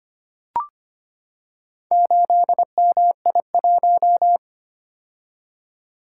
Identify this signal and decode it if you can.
Morse code, heard as a keyed tone, about 25 wpm, 700 Hz. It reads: 8MI1
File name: practice / 25wpm / Four Letter-Number Combinations - 25wpm